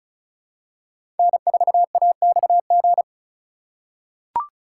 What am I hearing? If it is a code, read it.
Morse code, heard as a keyed tone, about 36 wpm, 690 Hz. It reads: N4AXG
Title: USA Call Signs - 5 Characters - 2x Character Spacing - 35wpm